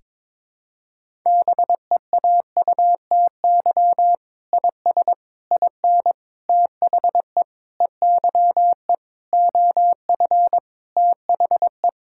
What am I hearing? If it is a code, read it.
Morse code, heard as a keyed tone, about 22 wpm, 700 Hz. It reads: BEAUTY IS IN THE EYE OF THE
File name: practice / 22wpm / Sentences with 8-Word Phrases - 22wpm